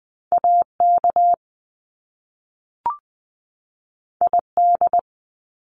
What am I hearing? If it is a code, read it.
Morse code, heard as a keyed tone, about 20 wpm, 700 Hz. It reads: AK ID
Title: US States 20wpm